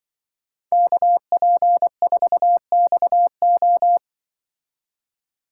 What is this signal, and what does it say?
Morse code, heard as a keyed tone, about 25 wpm, 700 Hz. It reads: KP4XO
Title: Speed Racing - USA 2x2 Call Signs 20wpm